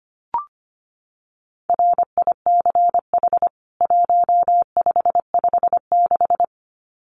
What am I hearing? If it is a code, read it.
Morse code, heard as a keyed tone, about 25 wpm, 700 Hz. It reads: RICH 1556
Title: CWT Exchange 25wpm